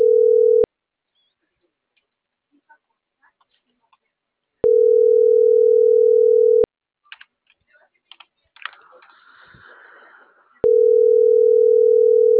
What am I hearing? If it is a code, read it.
Ringback tone (North America), 440 + 480 Hz.